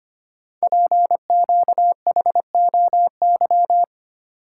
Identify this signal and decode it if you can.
Morse code, heard as a keyed tone, about 25 wpm, 700 Hz. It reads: PQHOY
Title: Code Groups - 5 Letters - 25wpm